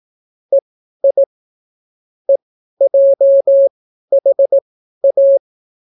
Morse code, heard as a keyed tone, about 18 wpm, spaced as for 12 wpm, 550 Hz. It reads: EI EJHA